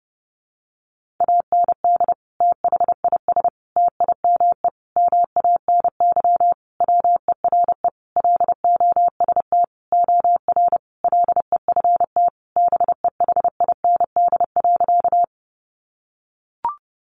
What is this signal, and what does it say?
Morse code, heard as a keyed tone, about 30 wpm, 700 Hz. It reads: AND THIS TIME MANY WERE LOST OR LEFT BEHIND.
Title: Sentences from Top 1400 Words 30wpm